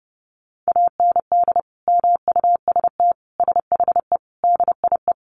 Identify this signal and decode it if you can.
Morse code, heard as a keyed tone, about 30 wpm, 700 Hz. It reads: AND MUST SHE DIE
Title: Sentences from Top 1300 Words 30wpm